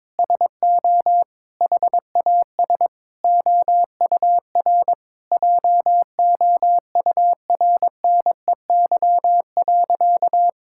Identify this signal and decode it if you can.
Morse code, heard as a keyed tone, about 21 wpm, 700 Hz. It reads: SO HAS OUR JOURNEY.
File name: Sentences with 9-Word Phrases - 22wpm